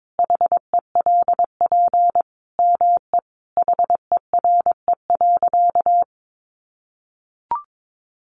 Morse code, heard as a keyed tone, about 22 wpm, 700 Hz. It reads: HELP ME HERE.
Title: Sentences from Top 200 Words 22wpm